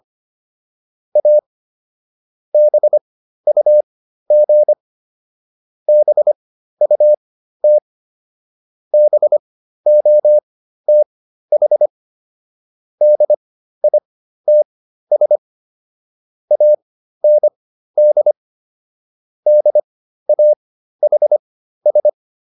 Morse code, heard as a keyed tone, about 25 wpm, spaced as for 13 wpm, 600 Hz. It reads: A BUG BUT BOTH DITS AND DAHS